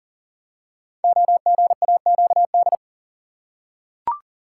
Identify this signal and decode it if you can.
Morse code, heard as a keyed tone, about 40 wpm, 700 Hz. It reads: OGAQD
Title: Code Groups - 5 Letters - 40wpm